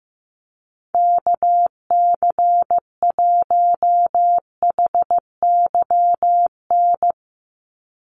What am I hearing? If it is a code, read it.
Morse code, heard as a keyed tone, about 15 wpm, 700 Hz. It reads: KC1HYN
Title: USA Call Signs 15wpm